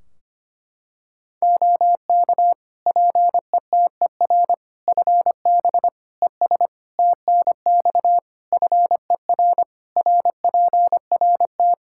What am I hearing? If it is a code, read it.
Morse code, heard as a keyed tone, about 25 wpm, 700 Hz. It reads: OK PETER FB ES TNX FER RPRT
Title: K1USN SST QSO Practice 25wpm